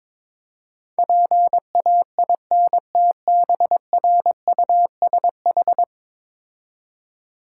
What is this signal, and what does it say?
Morse code, heard as a keyed tone, about 22 wpm, 700 Hz. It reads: PAINTBRUSH
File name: Difficult Words with Hints - 22wpm